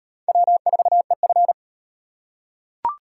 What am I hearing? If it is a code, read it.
Morse code, heard as a keyed tone, about 38 wpm, 700 Hz. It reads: W4EF